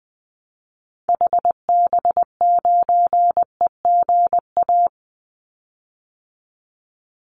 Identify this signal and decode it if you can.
Morse code, heard as a keyed tone, about 20 wpm, 700 Hz. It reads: HB9EGA
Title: Call Signs - 20wpm